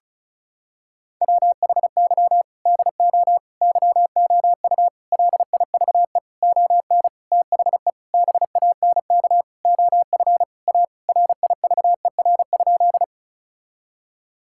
Morse code, heard as a keyed tone, about 34 wpm, 700 Hz. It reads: WHY DO YOU LIVE ON THE BANK OF A RIVER?
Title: Sentences from Top 1000 Words - Review Part 6 - 35wpm